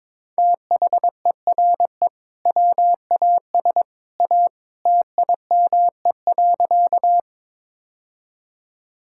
Morse code, heard as a keyed tone, about 22 wpm, 700 Hz. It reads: THERE WAS A TIME.
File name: practice / 22wpm / Sentences with 2-Word Phrases - 22wpm